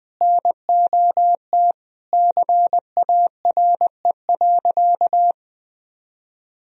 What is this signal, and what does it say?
Morse code, heard as a keyed tone, about 20 wpm, 700 Hz. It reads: NOT CARE.
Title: Sentences from Top 1000 Words - Review Part 3 - 20wpm